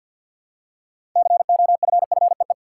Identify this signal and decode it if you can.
Morse code, heard as a keyed tone, about 52 wpm, 700 Hz. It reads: COFFEE
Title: Six Letter Words - 50wpm